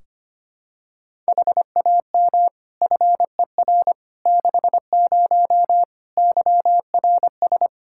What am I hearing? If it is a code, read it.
Morse code, heard as a keyed tone, about 25 wpm, 700 Hz. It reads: HAM FER 60 YRS
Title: K1USN SST QSO Practice 25wpm